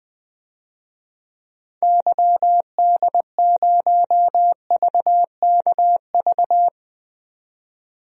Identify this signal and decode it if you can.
Morse code, heard as a keyed tone, about 20 wpm, 700 Hz. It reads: YD0VKV
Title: DXCC-100 Call Signs - 6 Characters - 20wpm